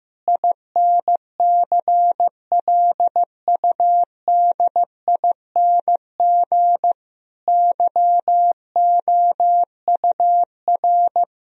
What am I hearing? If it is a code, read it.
Morse code, heard as a keyed tone, about 15 wpm, 700 Hz. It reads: INCLUDING YOUR
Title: Sentences from Top 1000 Words - Review Part 4 - 15wpm